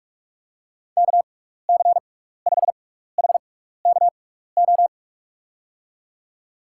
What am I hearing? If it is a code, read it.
Morse code, heard as a keyed tone, about 46 wpm, spaced as for 15 wpm, 700 Hz. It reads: KC5HKO
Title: USA Call Signs - 6 Characters - 6x Character Spacing - 45wpm